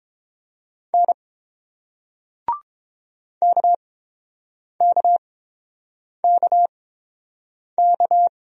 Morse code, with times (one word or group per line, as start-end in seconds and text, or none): 0.9-1.1 N
3.4-8.3 KKKK